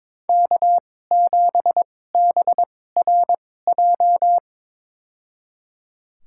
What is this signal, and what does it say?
Morse code, heard as a keyed tone, about 22 wpm, spaced as for 16 wpm, 700 Hz. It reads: K7BRJ
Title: USA Call Signs - 2x Character Spacing - 22wpm